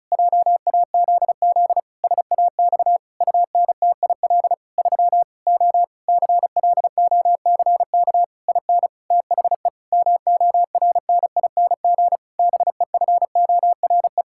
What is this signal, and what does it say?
Morse code, heard as a keyed tone, about 36 wpm, 700 Hz. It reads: JAZZ SAX UNTIL 3 O CLOCK IN THE MORNING BEFORE
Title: Pangram Challenge v2 - 35wpm